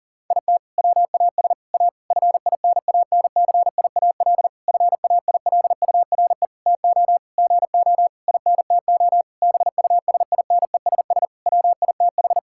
Morse code, heard as a keyed tone, about 40 wpm, 700 Hz. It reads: IT WAS A FINANCIAL FAILURE TO GO INTO BUSINESS WITH
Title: Sentences from Top 1000 Words - Review Part 6 - 40wpm